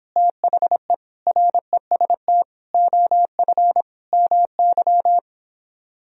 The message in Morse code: THE REST OF MY

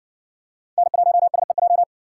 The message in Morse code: N1SEJ

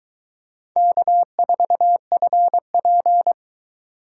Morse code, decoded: K4FP

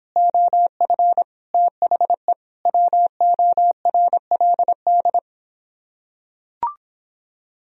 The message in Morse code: OF THE WORLD

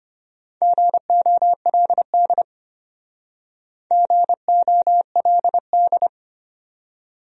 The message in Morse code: GOLD GOLD

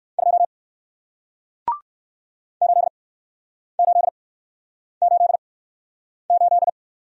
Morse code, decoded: 2 8888

